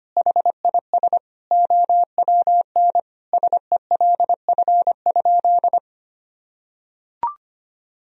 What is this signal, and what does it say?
Morse code, heard as a keyed tone, about 25 wpm, 700 Hz. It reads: HIS OWN SELF?
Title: Sentences from Top 1000 Words - Review Part 1 - 25wpm